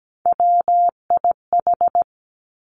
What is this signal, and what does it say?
Morse code, heard as a keyed tone, about 17 wpm, 700 Hz. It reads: WIH
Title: Three Challenging Letter-Number Combinations 17wpm